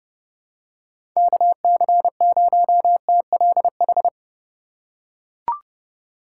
Morse code, heard as a keyed tone, about 30 wpm, 700 Hz. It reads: KC0TLH
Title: USA Call Signs - 3x Character Spacing - 30wpm